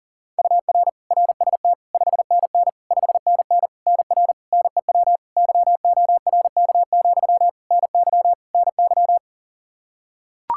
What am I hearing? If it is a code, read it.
Morse code, heard as a keyed tone, about 40 wpm, 700 Hz. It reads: UR RST 5NN 5NN NR NEW YORK, NY NY
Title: K1USN SST QSO Practice 40wpm